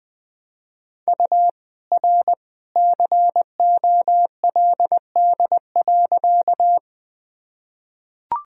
Morse code, heard as a keyed tone, about 20 wpm, 700 Hz. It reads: U R COLD.
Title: Sentences from Top 400 Words and Abbreviations 20wpm